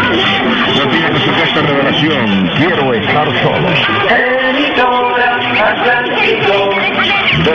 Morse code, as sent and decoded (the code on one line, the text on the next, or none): none